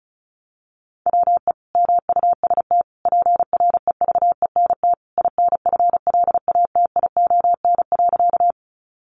Morse code, heard as a keyed tone, about 34 wpm, 700 Hz. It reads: WE MUST PREVENT INFLATION.
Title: Sentences from Top 1100 Words 35wpm